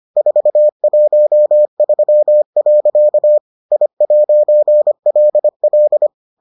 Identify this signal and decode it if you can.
Morse code, heard as a keyed tone, about 25 wpm, 600 Hz. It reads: 413. I'LL